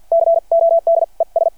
Morse code, heard as a keyed tone, about 51 wpm, spaced as for 39 wpm, 670 Hz. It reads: CODES
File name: archived / CRSnet